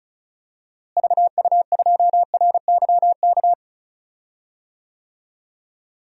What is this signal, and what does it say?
Morse code, heard as a keyed tone, about 36 wpm, 700 Hz. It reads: VU2RYK
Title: DXCC-100 Call Signs - 6 Characters - Repeated 3x - 35wpm